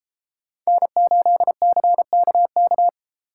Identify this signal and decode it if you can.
Morse code, heard as a keyed tone, about 33 wpm, 700 Hz. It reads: N8CKK